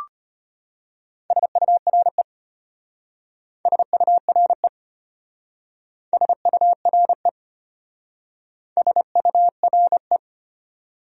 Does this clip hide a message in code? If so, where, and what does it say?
Yes: Morse code from 1.3 to 10.2 s, heard as a keyed tone, about 32 wpm, 700 Hz. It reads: SURE SURE SURE SURE